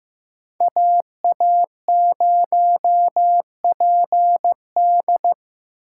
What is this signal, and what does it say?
Morse code, heard as a keyed tone, about 15 wpm, 700 Hz. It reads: AA0PD
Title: USA Call Signs - 5 Characters - 2x Character Spacing - 15wpm